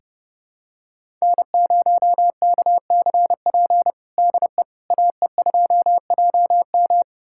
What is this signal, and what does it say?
Morse code, heard as a keyed tone, about 30 wpm, 700 Hz. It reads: N0KCP DE AE2JM
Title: USA Callsign DE Callsign 30wpm